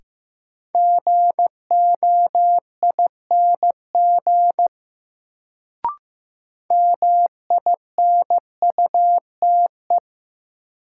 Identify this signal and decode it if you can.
Morse code, heard as a keyed tone, about 15 wpm, 700 Hz. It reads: GOING MINUTE
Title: Sentences from Top 500 Words - New Word Drill - 15wpm